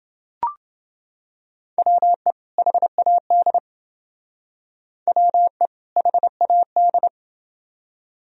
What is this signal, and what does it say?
Morse code, heard as a keyed tone, about 29 wpm, 700 Hz. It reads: WE HAD WE HAD